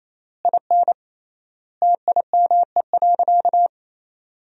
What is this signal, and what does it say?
Morse code, heard as a keyed tone, about 28 wpm, 700 Hz. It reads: IN TIME.
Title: 3x Word-Spacing - Sentences from Top 300 Words - 28wpm